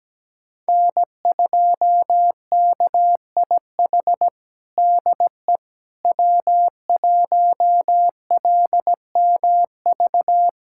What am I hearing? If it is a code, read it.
Morse code, heard as a keyed tone, about 17 wpm, 700 Hz. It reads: N2KIH DE W1LMV
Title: USA Callsign DE Callsign - 5 Characters - 17wpm